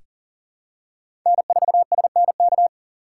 Morse code, heard as a keyed tone, about 40 wpm, 700 Hz. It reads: N4SNK
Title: USA Call Signs - 5 Characters - 6x Character Spacing - 40wpm